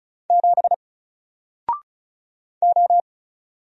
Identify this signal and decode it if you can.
Morse code, heard as a keyed tone, about 36 wpm, 700 Hz. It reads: 7 O
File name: Single-Letter-Number-35wpm